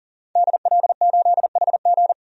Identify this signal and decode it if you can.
Morse code, heard as a keyed tone, about 40 wpm, 700 Hz. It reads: DL8HG
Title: DXCC-100 Call Signs - 5 Characters - 40wpm